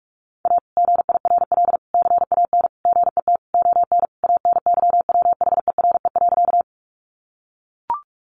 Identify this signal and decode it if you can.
Morse code, heard as a keyed tone, about 44 wpm, 700 Hz. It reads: A GIRL CAN GET ON ANYWHERE.
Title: Sentences from Top 1400 Words 45wpm